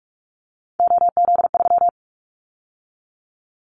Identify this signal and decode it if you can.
Morse code, heard as a keyed tone, about 44 wpm, 700 Hz. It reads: OZ3
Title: Callsign Prefixes for Top 50 DXCC - 45wpm